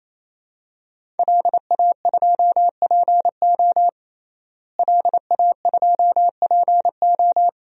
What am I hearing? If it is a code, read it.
Morse code, heard as a keyed tone, about 27 wpm, 700 Hz. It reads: LA2PO LA2PO
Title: DXCC-100 Call Signs - 5 Characters - Repeated 3x - 28wpm